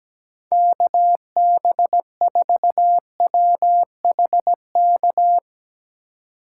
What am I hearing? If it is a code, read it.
Morse code, heard as a keyed tone, about 17 wpm, 700 Hz. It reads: KB4WHK
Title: USA Call Signs - 6 Characters - 3x Character Spacing - 17wpm